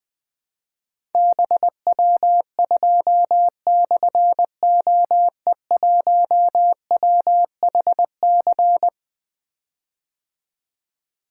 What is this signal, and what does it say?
Morse code, heard as a keyed tone, about 21 wpm, 700 Hz. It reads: BW2/OE1WHC
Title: Difficult Call Signs 20wpm